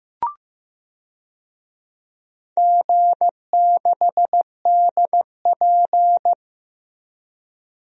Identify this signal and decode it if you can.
Morse code, heard as a keyed tone, about 16 wpm, 700 Hz. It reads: G6DP
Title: DXCC-100 Call Signs - 4 Characters - Repeated 2x - 15wpm